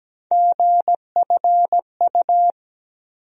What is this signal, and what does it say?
Morse code, heard as a keyed tone, about 17 wpm, 700 Hz. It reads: GFU
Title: USA Call Sign Postfixes - 3 Letters 17wpm